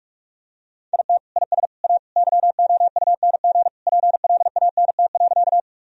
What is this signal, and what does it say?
Morse code, heard as a keyed tone, about 41 wpm, 700 Hz. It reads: IT IS A YOUNG PLANT.
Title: Sentences from Top 1000 Words - Review Part 1 - 45wpm